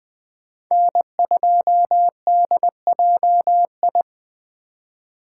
Morse code, heard as a keyed tone, about 21 wpm, 700 Hz. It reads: N2DJI
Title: Speed Racing - USA Call Signs 17wpm